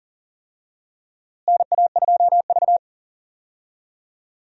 Morse code, heard as a keyed tone, about 40 wpm, 700 Hz. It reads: NA2V
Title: USA Call Signs - 4 Characters - Repeated 3x - 40wpm